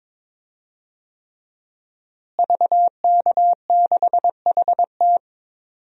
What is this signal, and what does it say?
Morse code, heard as a keyed tone, about 22 wpm, 700 Hz. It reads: VK6HT
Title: DXCC-100 Call Signs - 5 Characters - Repeated 3x - 22wpm